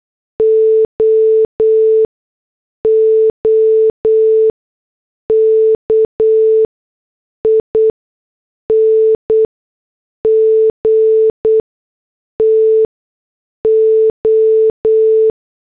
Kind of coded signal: Morse code